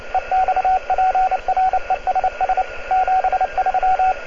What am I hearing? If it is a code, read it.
Morse code, heard as a keyed tone, about 29 wpm, 750 Hz, with radio noise. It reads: EXPRESS 73